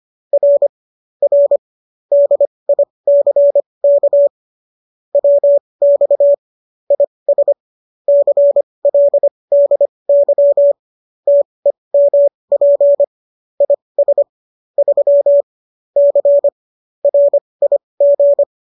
Morse code, heard as a keyed tone, about 25 wpm, spaced as for 20 wpm, 570 Hz. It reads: R R DICK WX IS CLDY TEMP IS 3 C RIG